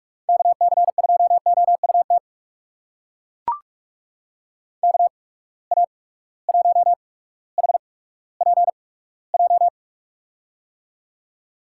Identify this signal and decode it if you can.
Morse code, heard as a keyed tone, about 42 wpm, spaced as for 19 wpm, 700 Hz. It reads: KC2OUT KA1HPJ
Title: USA Call Signs - 6 Characters - 8x Character Spacing - 45wpm